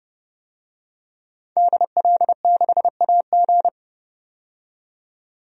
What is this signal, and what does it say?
Morse code, heard as a keyed tone, about 30 wpm, 700 Hz. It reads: DL6AG
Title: DXCC-100 Call Signs - 5 Characters - 30wpm